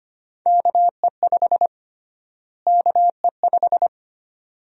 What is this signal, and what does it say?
Morse code, heard as a keyed tone, about 25 wpm, 700 Hz. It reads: KE5 KE5
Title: USA Call Sign Prefixes - Repeated 2x - 25wpm